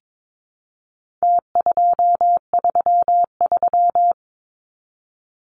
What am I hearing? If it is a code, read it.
Morse code, heard as a keyed tone, about 22 wpm, 700 Hz. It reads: T233
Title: Numbers - Contest Style - 4-Digit 22wpm